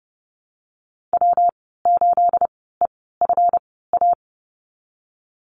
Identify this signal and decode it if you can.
Morse code, heard as a keyed tone, about 30 wpm, spaced as for 17 wpm, 700 Hz. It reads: W8EFA
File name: Lesson 37g - Letter x - Call Signs 30wpm